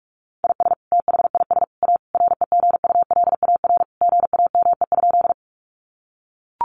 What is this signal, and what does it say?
Morse code, heard as a keyed tone, about 44 wpm, 700 Hz. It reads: IS THIS A REGULAR GAME?
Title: Sentences from Top 900 Words 45wpm